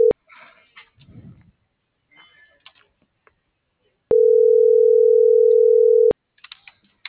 Ringback tone (North America), 440 + 480 Hz.